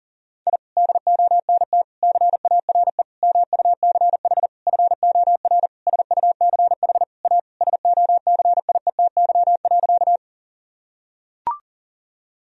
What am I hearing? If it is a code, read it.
Morse code, heard as a keyed tone, about 40 wpm, 700 Hz. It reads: I DONT CARE MUCH FOR SUCH A SOCIETY.
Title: Sentences from Top 1000 Words - Review Part 4 - 40wpm